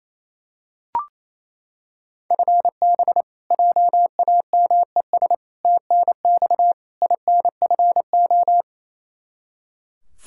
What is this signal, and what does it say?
Morse code, heard as a keyed tone, about 27 wpm, 700 Hz. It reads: FB JAMES TNX INFO
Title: Practice QSO - Exchange 3 - More Info - 28wpm